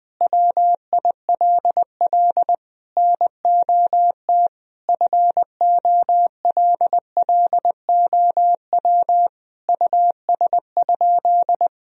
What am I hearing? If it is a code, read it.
Morse code, heard as a keyed tone, about 21 wpm, 700 Hz. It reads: WILL NOT FOLLOW US?
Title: Sentences from Top 1000 Words - Review Part 1 - 20wpm